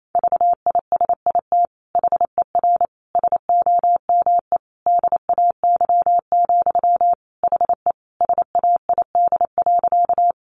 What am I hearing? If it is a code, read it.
Morse code, heard as a keyed tone, about 28 wpm, 700 Hz. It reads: VISIT HER SOME DAY, HE SAID.
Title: Sentences from Top 900 Words 28wpm